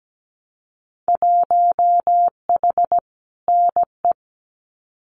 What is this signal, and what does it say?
Morse code, heard as a keyed tone, about 17 wpm, 700 Hz. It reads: 1H NE
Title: Winter Field Day Exchanges Encore 17wpm